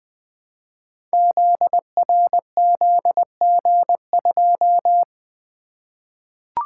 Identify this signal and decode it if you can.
Morse code, heard as a keyed tone, about 20 wpm, 700 Hz. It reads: ZRZG2
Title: Code Groups - 5 Letter:Numbers - 20wpm